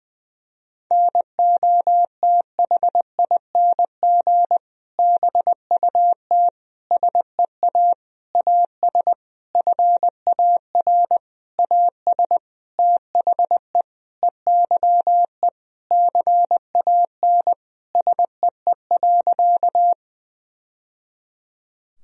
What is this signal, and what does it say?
Morse code, heard as a keyed tone, about 20 wpm, 700 Hz. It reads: NOTHING BUT SEA AS FAR AS THE EYE CAN SEE.
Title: Sentences from Top 1000 Words - Review Part 1 - 20wpm